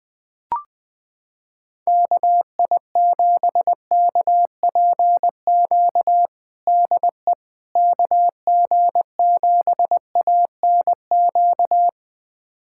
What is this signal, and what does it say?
Morse code, heard as a keyed tone, about 20 wpm, 700 Hz. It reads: KI7KPQ DE KG7ANQ
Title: USA Callsign DE Callsign - 6 Characters - 20wpm